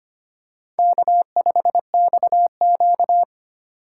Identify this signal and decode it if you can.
Morse code, heard as a keyed tone, about 25 wpm, 700 Hz. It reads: K5XQ
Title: K1USN SST QSO Practice - 25wpm